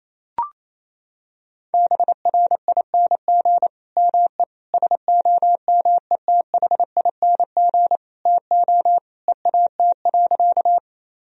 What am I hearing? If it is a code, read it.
Morse code, heard as a keyed tone, about 28 wpm, 700 Hz. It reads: BRING ME SOMETHING TO EAT.
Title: Sentences from Top 1000 Words - Review Part 3 - 28wpm